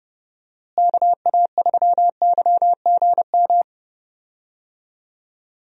Morse code, heard as a keyed tone, about 30 wpm, 700 Hz. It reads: KA3YGM
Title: USA Call Signs - 6 Characters - Repeated 3x - 30wpm